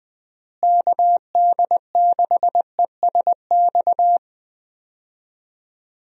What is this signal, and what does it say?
Morse code, heard as a keyed tone, about 20 wpm, 700 Hz. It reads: KD6ESX
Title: USA Call Signs - 6 Characters - Repeated 2x - 20wpm